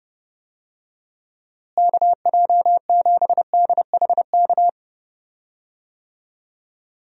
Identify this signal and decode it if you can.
Morse code, heard as a keyed tone, about 30 wpm, 700 Hz. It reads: KJ7DHK
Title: USA Call Signs - 6 Characters - Repeated 2x - 30wpm